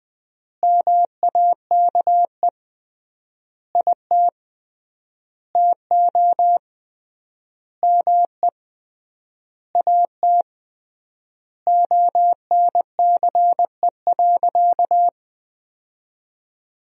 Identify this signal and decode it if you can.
Morse code, heard as a keyed tone, about 20 wpm, 700 Hz. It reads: MAKE IT TO ME AT ONCE.
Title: 3x Word-Spacing - Sentences from Top 500 Words - 20wpm